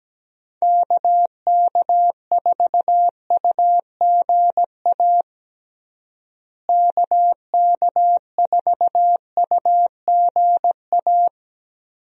Morse code, heard as a keyed tone, about 17 wpm, 700 Hz. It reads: KK4UGA KK4UGA
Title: USA Call Signs - 6 Characters - Repeated 3x - 17wpm